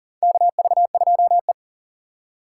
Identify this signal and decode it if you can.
Morse code, heard as a keyed tone, about 40 wpm, 700 Hz. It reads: KV2E
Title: USA Call Signs - 4 Characters - 3x Character Spacing - 40wpm